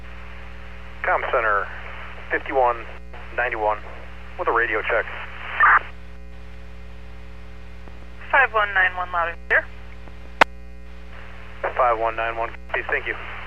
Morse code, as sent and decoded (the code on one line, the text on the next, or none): none